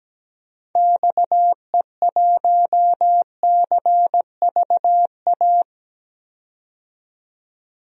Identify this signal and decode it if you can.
Morse code, heard as a keyed tone, about 17 wpm, 700 Hz. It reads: XE1CVA
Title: DXCC-100 Call Signs - 6 Characters - 17wpm